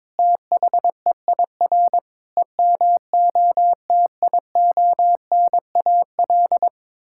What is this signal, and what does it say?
Morse code, heard as a keyed tone, about 22 wpm, 700 Hz. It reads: THEIR EMOTIONAL
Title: Sentences from Top 1000 Words - Review Part 5 - 22wpm